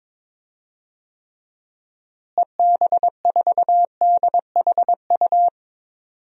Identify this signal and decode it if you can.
Morse code, heard as a keyed tone, about 22 wpm, 700 Hz. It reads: EB4DHU
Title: DXCC-100 Call Signs - 6 Characters - Repeated 2x - 22wpm